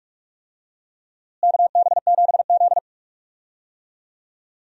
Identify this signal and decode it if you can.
Morse code, heard as a keyed tone, about 46 wpm, 700 Hz. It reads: KB7Z
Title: USA Call Signs - 4 Characters - Repeated 2x - 45wpm